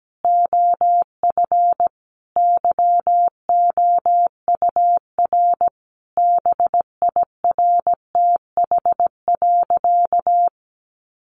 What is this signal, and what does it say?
Morse code, heard as a keyed tone, about 17 wpm, 700 Hz. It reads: OF YOUR BIRTH.
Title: Sentences from Top 1300 Words 17wpm